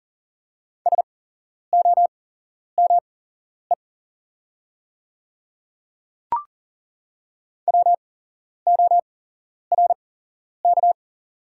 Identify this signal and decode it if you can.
Morse code, heard as a keyed tone, about 40 wpm, spaced as for 11 wpm, 700 Hz. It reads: SOME WORK
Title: 8x Character Spacing - Top 200 Words - 40wpm